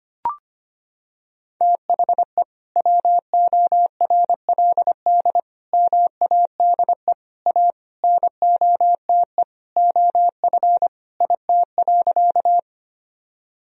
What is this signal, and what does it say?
Morse code, heard as a keyed tone, about 25 wpm, 700 Hz. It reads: THE WORLD MADE A NOTE OF IT.
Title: Sentences from Top 1000 Words - Review Part 2 - 25wpm